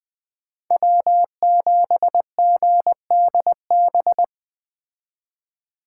Morse code, heard as a keyed tone, about 21 wpm, 700 Hz. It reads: W7GDB
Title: Speed Racing - USA Call Signs 17wpm